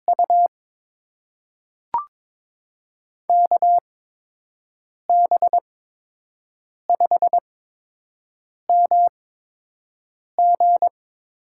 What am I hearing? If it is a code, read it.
Morse code, heard as a keyed tone, about 22 wpm, spaced as for 6 wpm, 700 Hz. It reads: U KB5MG
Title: USA Call Signs - 6 Characters - 8x Character Spacing - 22wpm